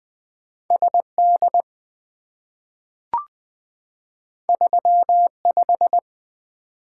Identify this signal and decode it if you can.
Morse code, heard as a keyed tone, about 20 wpm, 700 Hz. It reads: SD 35
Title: Two Letter-Number Combinations - 20wpm